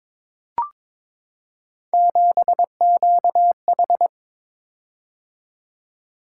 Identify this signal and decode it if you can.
Morse code, heard as a keyed tone, about 22 wpm, 700 Hz. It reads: 7QH
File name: practice / 22wpm / Three Letter-Number Combinations - 22wpm